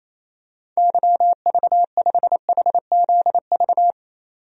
Morse code, heard as a keyed tone, about 28 wpm, 700 Hz. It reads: YV5HZV
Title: DXCC-100 Call Signs - 6 Characters - 28wpm